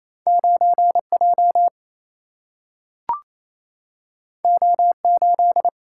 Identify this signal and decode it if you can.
Morse code, heard as a keyed tone, about 28 wpm, 700 Hz. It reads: 9 J O 8
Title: Single-Letter Single-Number 28wpm